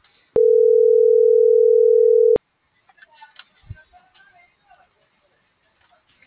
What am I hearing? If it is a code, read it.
Ringback tone (North America), 440 + 480 Hz.
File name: CENTRAL TERRESTRE / out ramon.lamorena.2024